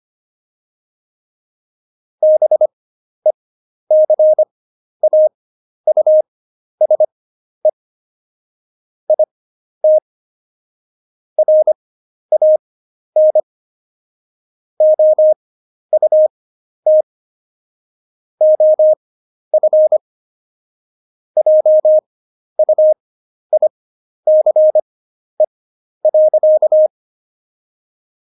Morse code, heard as a keyed tone, about 25 wpm, spaced as for 11 wpm, 620 Hz. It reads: BECAUSE IT RAN OUT OF JUICE.
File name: Jokes-25-char-10-eff